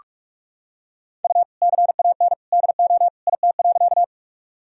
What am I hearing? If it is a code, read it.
Morse code, heard as a keyed tone, about 46 wpm, 700 Hz. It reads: U CAN DO IT.
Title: Sentences from Top 100 Words and Abbreviations 45wpm